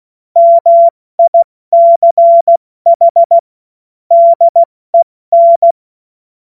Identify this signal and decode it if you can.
Morse code, heard as a keyed tone, about 16 wpm, 680 Hz. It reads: MICH DEN
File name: Morsekurs / Witze / Witz 06 Februar 2024 70 F80